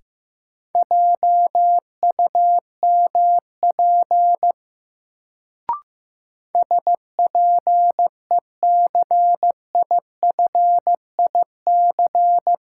Morse code, heard as a keyed tone, about 15 wpm, 700 Hz. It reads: JUMP SPECIFIC
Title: Sentences from Top 1300 Words - New Word Drill Alt - 15wpm